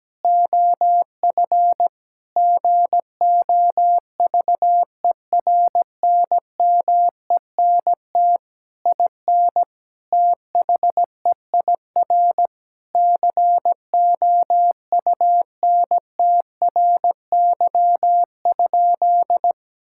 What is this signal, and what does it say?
Morse code, heard as a keyed tone, about 17 wpm, 700 Hz. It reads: OF GOVERNMENT IN THEIR COUNTRY?
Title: Sentences from Top 1000 Words - Review Part 5 - 17wpm